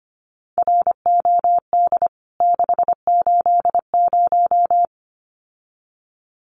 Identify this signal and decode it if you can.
Morse code, heard as a keyed tone, about 25 wpm, 700 Hz. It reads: ROD 680
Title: CWT Exchange 25wpm